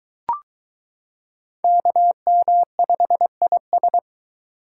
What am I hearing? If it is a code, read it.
Morse code, heard as a keyed tone, about 23 wpm, 700 Hz. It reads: KM5IS